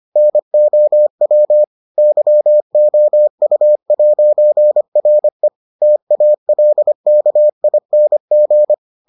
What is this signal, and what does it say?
Morse code, heard as a keyed tone, about 25 wpm, 600 Hz. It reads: NOW YOU'RE TALKING